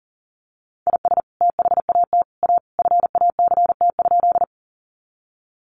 Morse code, heard as a keyed tone, about 37 wpm, 700 Hz. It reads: IS THAT A FACT?
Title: Sentences from Top 500 Words 40wpm